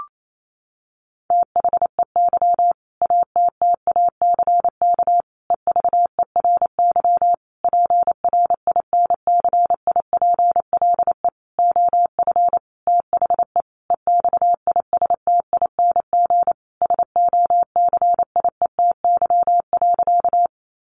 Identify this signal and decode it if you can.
Morse code, heard as a keyed tone, about 28 wpm, 700 Hz. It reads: THEY ATTACK EVERY PRINCIPLE OF THE EXISTING SOCIETY.
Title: Sentences from Top 1500 Words 28wpm